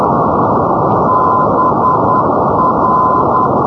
Morse code, heard as a keyed tone, about 10 wpm, 1020 Hz, with noise. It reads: BR
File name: europe / hng / hng-720 ndb budapest br-395-np-aut-poellau-2011-rb